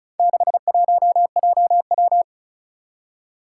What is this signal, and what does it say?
Morse code, heard as a keyed tone, about 36 wpm, 700 Hz. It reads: 61JW
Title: Four Challenging Letter-Number Combinations 35wpm